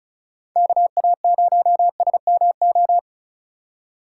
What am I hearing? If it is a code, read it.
Morse code, heard as a keyed tone, about 34 wpm, 700 Hz. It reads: KA0SMO
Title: USA Call Signs - 4x Character Spacing - 35wpm